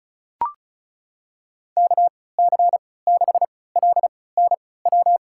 Morse code, heard as a keyed tone, about 34 wpm, spaced as for 20 wpm, 690 Hz. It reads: KC6LNW
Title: USA Call Signs - 6 Characters - 3x Character Spacing - 35wpm